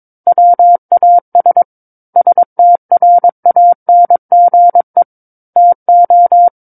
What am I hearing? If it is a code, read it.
Morse code, heard as a keyed tone, about 22 wpm, 700 Hz. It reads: WAS STRANGE TO
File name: Book One / 13-Chapter 13 - 22 WPM